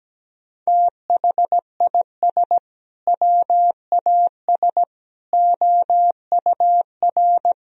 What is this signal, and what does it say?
Morse code, heard as a keyed tone, about 17 wpm, 700 Hz. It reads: THIS WAS OUR
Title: Sentences from Top 1000 Words - Review Part 1 - 17wpm